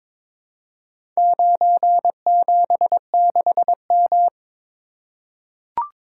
Morse code, heard as a keyed tone, about 23 wpm, 700 Hz. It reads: 976M
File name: practice / 22wpm / Four Letter-Number Combinations - 22wpm